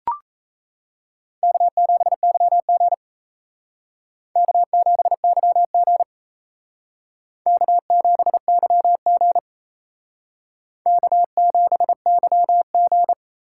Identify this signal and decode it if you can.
Morse code, heard as a keyed tone, about 35 wpm, 700 Hz. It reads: K7YG K7YG K7YG K7YG
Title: Speed Racing - USA 1x2 Call Signs 28wpm